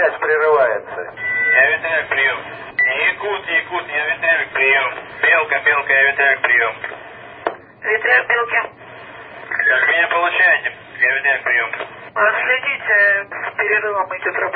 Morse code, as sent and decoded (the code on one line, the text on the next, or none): none